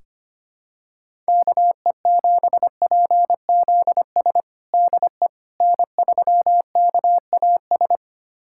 Morse code, heard as a keyed tone, about 25 wpm, 700 Hz. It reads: KE7PZS DE N3KAS
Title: USA Callsign DE Callsign 25wpm